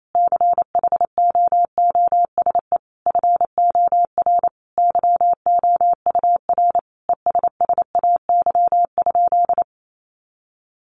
Morse code, heard as a keyed tone, about 28 wpm, 700 Hz. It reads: CHOOSE FOR YOUR ESSAY?
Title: Sentences from Top 1500 Words 28wpm